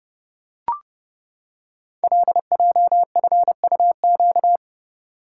Morse code, heard as a keyed tone, about 30 wpm, 700 Hz. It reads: LJFUQ
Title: Code Groups - 5 Letters - 30wpm